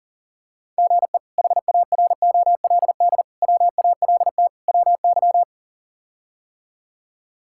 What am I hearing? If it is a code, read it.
Morse code, heard as a keyed tone, about 40 wpm, 700 Hz. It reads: GE HAROLD WALT WY
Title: K1USN SST QSO Practice - 40wpm